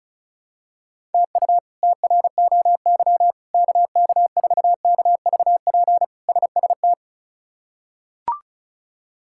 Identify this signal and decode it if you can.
Morse code, heard as a keyed tone, about 36 wpm, 700 Hz. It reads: TU TROY KK4KVP SST
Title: K1USN SST QSO Practice - 35wpm